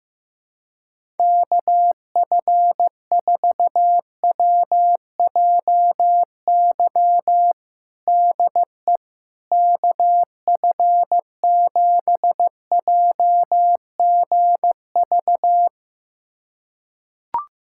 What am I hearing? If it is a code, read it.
Morse code, heard as a keyed tone, about 15 wpm, 700 Hz. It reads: KF4WJY DE KF7JGV
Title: USA Callsign DE Callsign 15wpm